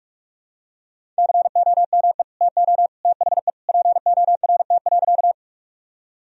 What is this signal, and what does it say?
Morse code, heard as a keyed tone, about 45 wpm, 700 Hz. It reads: COME TO THE PORT.
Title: Sentences from Top 1000 Words - Review Part 1 - 45wpm